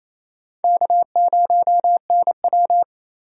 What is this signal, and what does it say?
Morse code, heard as a keyed tone, about 28 wpm, 700 Hz. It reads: K0NW
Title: K1USN SST QSO Practice - 28wpm